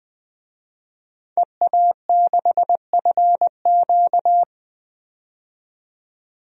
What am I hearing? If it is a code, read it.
Morse code, heard as a keyed tone, about 20 wpm, 700 Hz. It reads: EA6FQ
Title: DXCC-100 Call Signs - 5 Characters - 20wpm